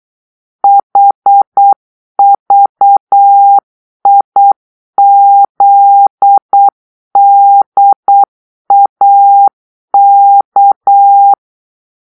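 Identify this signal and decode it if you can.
Morse code, heard as a keyed tone, about 8 wpm, 800 Hz. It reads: HVIZDAK
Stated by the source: Morse-code cipher made by a player team for the Kachna puzzle game.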